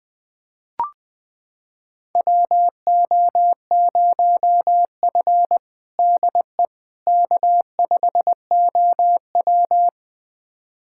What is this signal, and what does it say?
Morse code, heard as a keyed tone, about 20 wpm, 700 Hz. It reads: WO0F DE K5OW
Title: USA Callsign DE Callsign - 4 Characters - 20wpm